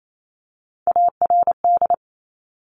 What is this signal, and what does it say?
Morse code, heard as a keyed tone, about 28 wpm, 700 Hz. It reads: ARD
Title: Top Three Letter Combinations in Words 28wpm